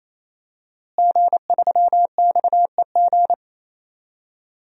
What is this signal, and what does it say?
Morse code, heard as a keyed tone, about 28 wpm, 700 Hz. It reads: G3XEG
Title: DXCC-100 Call Signs - 5 Characters - Repeated 3x - 28wpm